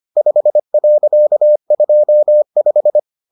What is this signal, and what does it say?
Morse code, heard as a keyed tone, about 25 wpm, 600 Hz. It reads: 5.25